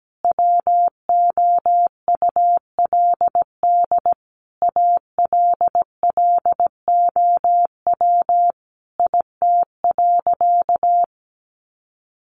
Morse code, heard as a keyed tone, about 17 wpm, 700 Hz. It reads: WOULD ALLOW IT.
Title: Sentences from Top 700 Words 17wpm